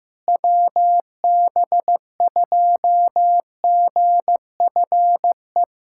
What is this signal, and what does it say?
Morse code, heard as a keyed tone, about 15 wpm, 700 Hz. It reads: WB2GFE
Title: USA Callsign DE Callsign 15wpm